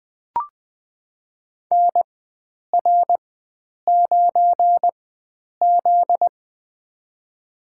Morse code, heard as a keyed tone, about 20 wpm, spaced as for 9 wpm, 700 Hz. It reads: NR9Z
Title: USA Call Signs - 4x Character Spacing - 20wpm